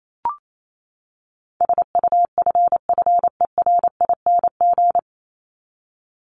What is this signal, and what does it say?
Morse code, heard as a keyed tone, about 28 wpm, 700 Hz. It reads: SUFFERING
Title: ING Suffix - Top 100 Words with Base Words 28wpm